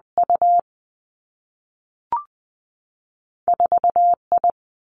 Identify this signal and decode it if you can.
Morse code, heard as a keyed tone, about 19 wpm, 700 Hz. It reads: U 4 I
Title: Two Challenging Letter-Number Combinations 20wpm